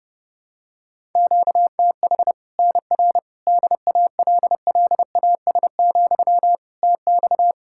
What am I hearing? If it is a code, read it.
Morse code, heard as a keyed tone, about 30 wpm, 700 Hz. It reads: QTH NR DALLAS, TX
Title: Practice QSO - Exchange 1 - Beginning - 30wpm